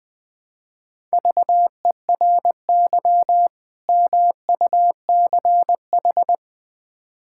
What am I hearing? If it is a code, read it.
Morse code, heard as a keyed tone, about 20 wpm, 700 Hz. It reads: VERY MUCH
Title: Build A Phrase - 5-Word Phrases - 20wpm